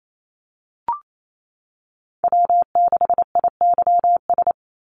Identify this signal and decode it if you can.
Morse code, heard as a keyed tone, about 28 wpm, 700 Hz. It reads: W6IYS
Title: Call Signs - 28wpm